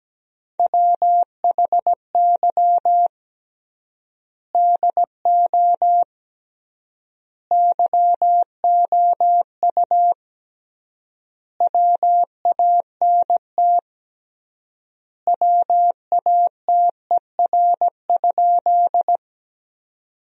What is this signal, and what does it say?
Morse code, heard as a keyed tone, about 17 wpm, 700 Hz. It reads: WHY DO YOU WANT WATER?
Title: 3x Word-Spacing - Sentences from Top 200 Words - 17wpm